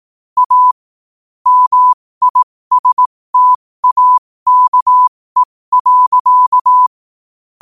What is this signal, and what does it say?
Morse code, heard as a keyed tone, about 18 wpm, spaced as for 15 wpm, 1000 Hz. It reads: A MISTAKE.
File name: qotdfiles / 15wpm